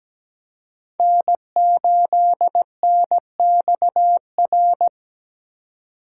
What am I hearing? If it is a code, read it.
Morse code, heard as a keyed tone, about 17 wpm, 700 Hz. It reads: N8NXR
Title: USA Call Signs - 5 Characters - 2x Character Spacing - 17wpm